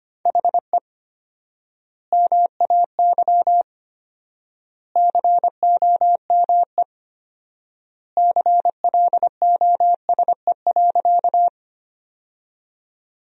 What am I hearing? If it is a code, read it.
Morse code, heard as a keyed tone, about 25 wpm, 700 Hz. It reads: HE MAY COME CLOSE.
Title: 4x Word-Spacing - Sentences from Top 300 Words - 25wpm